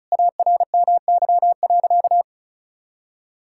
Morse code, heard as a keyed tone, about 34 wpm, 700 Hz. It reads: ARMY.
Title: Sentences from Top 1000 Words - Review Part 5 - 35wpm